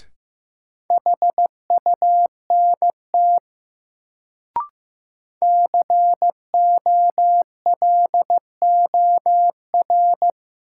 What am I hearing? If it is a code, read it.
Morse code, heard as a keyed tone, about 15 wpm, 700 Hz. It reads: HUNT COLOR
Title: Sentences from Top 1600 Words - New Word Drill Alt - 15wpm